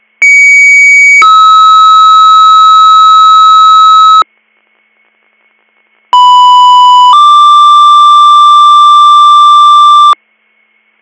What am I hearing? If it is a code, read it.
Two-tone page, heard as a one-second tone, then a three-second tone, about 2250 then 1290 Hz, about 990 then 1180 Hz. It is heard from 0.2 to 10.1 s.